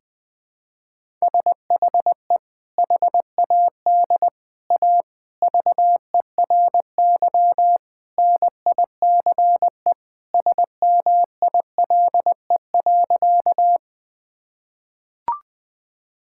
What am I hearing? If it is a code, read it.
Morse code, heard as a keyed tone, about 20 wpm, 700 Hz. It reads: SHE HAD A VERY NICE SMILE.
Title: Sentences with 2-Word Phrases - 20wpm